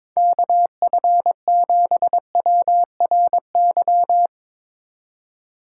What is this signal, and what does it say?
Morse code, heard as a keyed tone, about 21 wpm, 700 Hz. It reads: KF7WRY
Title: USA Call Signs - 6 Characters - 2x Character Spacing - 22wpm